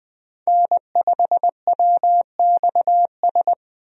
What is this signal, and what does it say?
Morse code, heard as a keyed tone, about 20 wpm, 700 Hz. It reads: N5WXS
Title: USA Call Signs - 5 Characters - 20wpm